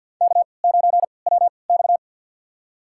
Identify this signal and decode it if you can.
Morse code, heard as a keyed tone, about 51 wpm, spaced as for 28 wpm, 700 Hz. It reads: K9WX
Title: USA Call Signs - 4 Characters - 3x Character Spacing - 50wpm